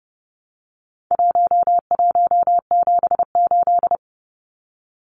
Morse code, heard as a keyed tone, about 30 wpm, 700 Hz. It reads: 1178
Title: Numbers - 4-Digit 30wpm